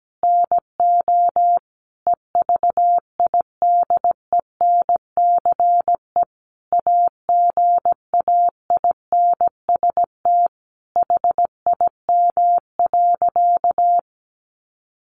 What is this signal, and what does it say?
Morse code, heard as a keyed tone, about 17 wpm, 700 Hz. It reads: NO EVIDENCE AGAINST HIM.
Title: Sentences from Top 1300 Words 17wpm